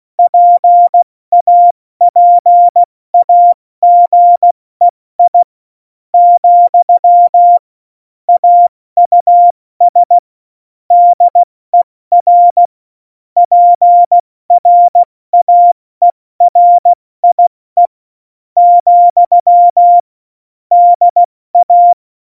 Morse code, heard as a keyed tone, about 16 wpm, 700 Hz. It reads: PAPAGEI , AUS DER PRAERIE , DA
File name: Morsekurs / Witze / Witz 04. September 2018, 70 F80